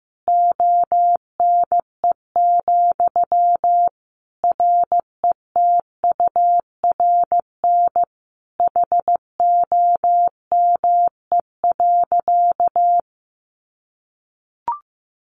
Morse code, heard as a keyed tone, about 15 wpm, 700 Hz. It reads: ONE, RETURN HOME.
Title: Sentences from Top 800 Words 15wpm